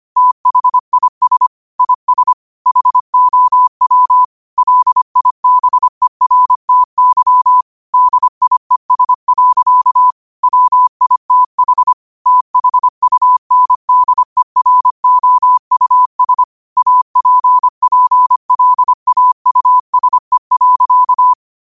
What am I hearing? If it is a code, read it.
Morse code, heard as a keyed tone, about 25 wpm, 1000 Hz. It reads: THIS IS HOW LIBERTY DIES. WITH THUNDEROUS APPLAUSE.
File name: qotdfiles / 25wpm